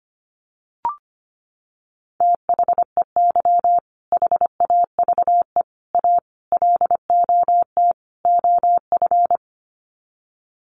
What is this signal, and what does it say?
Morse code, heard as a keyed tone, about 25 wpm, 700 Hz. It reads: THEY HAVE A LOT OF
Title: Sets of 5 Words Encore2 25wpm